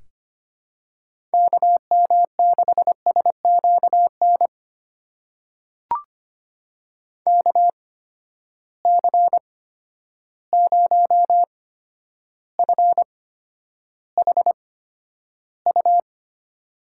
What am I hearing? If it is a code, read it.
Morse code, heard as a keyed tone, about 25 wpm, spaced as for 11 wpm, 700 Hz. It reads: KM6SQN KC0FHU
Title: USA Call Signs - 6 Characters - 8x Character Spacing - 25wpm